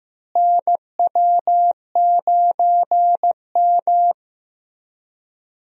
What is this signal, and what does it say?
Morse code, heard as a keyed tone, about 15 wpm, 700 Hz. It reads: NW9M
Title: USA Call Signs - 4 Characters - Repeated 3x - 15wpm